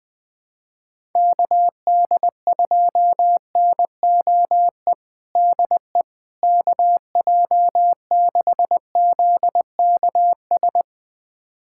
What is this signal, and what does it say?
Morse code, heard as a keyed tone, about 20 wpm, 700 Hz. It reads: KD2NOE DE KJ6ZKS
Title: USA Callsign DE Callsign - 6 Characters - 20wpm